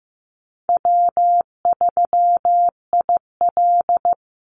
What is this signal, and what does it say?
Morse code, heard as a keyed tone, about 15 wpm, 700 Hz. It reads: W3IL
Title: Easy Call Signs Repeated 3x - 15wpm